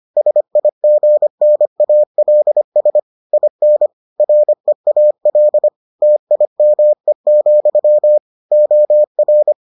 Morse code, heard as a keyed tone, about 25 wpm, 600 Hz. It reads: SIGNALS IN REAL TIME, OR